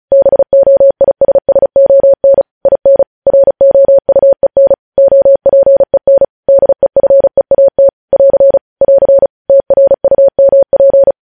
Morse code, heard as a keyed tone, about 36 wpm, 550 Hz. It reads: BOISSON IN ROUEN OPEN DEFEAT <AR> <AR> TRUMP
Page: news headlines in morse code at 35 wpm